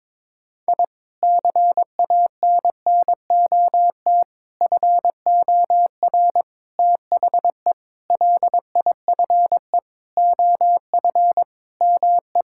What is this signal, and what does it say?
Morse code, heard as a keyed tone, about 22 wpm, 700 Hz. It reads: I CANNOT FOR THE LIFE OF ME
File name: practice / 22wpm / Sentences with 8-Word Phrases - 22wpm